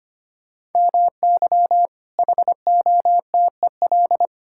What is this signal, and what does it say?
Morse code, heard as a keyed tone, about 25 wpm, 700 Hz. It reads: MY HOTEL
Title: Sentences from Top 1600 Words 25wpm